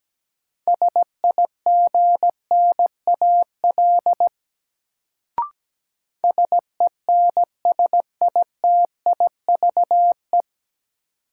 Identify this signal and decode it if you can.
Morse code, heard as a keyed tone, about 17 wpm, 700 Hz. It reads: SIGNAL SENSITIVE
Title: Sentences from Top 1400 Words - New Word Drill Alt - 17wpm